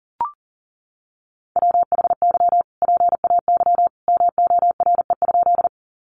Morse code, heard as a keyed tone, about 37 wpm, 700 Hz. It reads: WHY PAY MORE?
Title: Sentences from Top 700 Words 40wpm